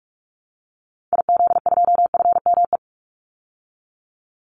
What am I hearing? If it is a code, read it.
Morse code, heard as a keyed tone, about 44 wpm, 700 Hz. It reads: IZ2FME
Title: Call Signs Repeated 2x - 45wpm